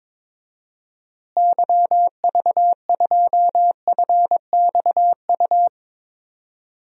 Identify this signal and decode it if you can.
Morse code, heard as a keyed tone, about 22 wpm, 700 Hz. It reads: YV2FXU
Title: DXCC-100 Call Signs - 6 Characters - Repeated 2x - 22wpm